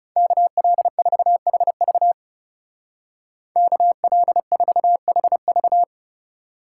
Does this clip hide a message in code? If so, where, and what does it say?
Yes: Morse code from 0.2 to 5.8 s, heard as a keyed tone, about 38 wpm, 700 Hz. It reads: KL4HV KL4HV